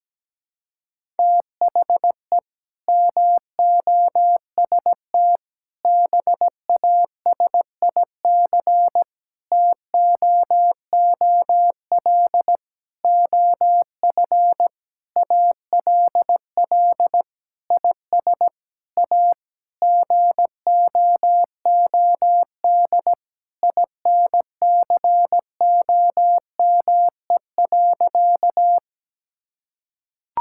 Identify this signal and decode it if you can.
Morse code, heard as a keyed tone, about 17 wpm, 700 Hz. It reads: THE MOST BASIC TOOL OF ALL IS A GOOD INCOME.
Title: Sentences from Top 1000 Words - Review Part 6 - 17wpm